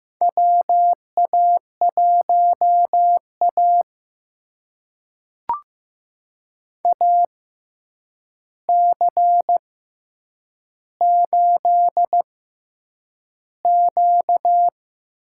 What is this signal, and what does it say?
Morse code, heard as a keyed tone, about 15 wpm, spaced as for 8 wpm, 700 Hz. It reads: WA1A AC8Q
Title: USA Call Signs - 4 Characters - 6x Character Spacing - 15wpm